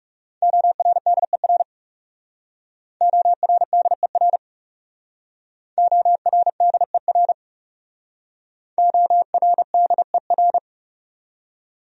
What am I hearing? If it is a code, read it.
Morse code, heard as a keyed tone, about 40 wpm, 700 Hz. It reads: ORDER ORDER ORDER ORDER